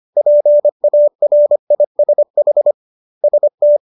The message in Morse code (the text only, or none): PARISH ST